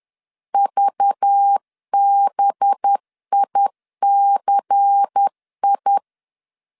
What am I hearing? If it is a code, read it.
Morse code, heard as a keyed tone, about 10 wpm, 800 Hz. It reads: VBICI
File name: CW / DET / 1 6 1500C S C 10wpm 4252